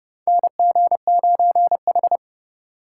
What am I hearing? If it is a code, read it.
Morse code, heard as a keyed tone, about 31 wpm, 700 Hz. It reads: NG9H
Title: USA Call Signs - 4 Characters - 3x Character Spacing - 30wpm